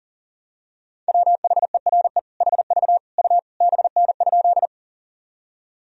Morse code, heard as a keyed tone, about 40 wpm, 700 Hz. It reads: WHERE HV U BN?
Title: Sentences from Top 200 Words and Abbreviations 40wpm